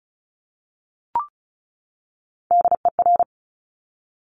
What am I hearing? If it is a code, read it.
Morse code, heard as a keyed tone, about 36 wpm, 700 Hz. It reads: DER
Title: Top Three Letter Combinations in Words 35wpm